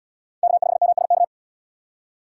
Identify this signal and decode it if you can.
Morse code, heard as a keyed tone, about 77 wpm, 700 Hz. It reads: K5MSK